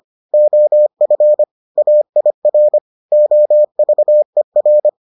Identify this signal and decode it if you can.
Morse code, heard as a keyed tone, about 25 wpm, 600 Hz. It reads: OF AIR OVER